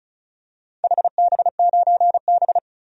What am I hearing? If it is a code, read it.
Morse code, heard as a keyed tone, about 34 wpm, 700 Hz. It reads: HB9B
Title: DXCC-100 Call Signs - 4 Characters - 35wpm